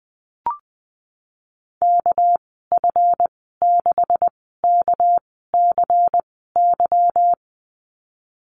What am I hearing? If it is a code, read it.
Morse code, heard as a keyed tone, about 20 wpm, spaced as for 15 wpm, 700 Hz. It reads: KF6KCY
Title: 238 Lesson 38g - Letter q - Call Signs 20wpm